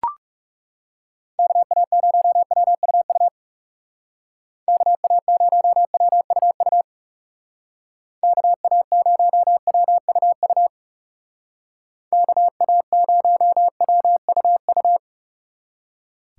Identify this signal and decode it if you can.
Morse code, heard as a keyed tone, about 39 wpm, 700 Hz. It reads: KA0WUU KA0WUU KA0WUU KA0WUU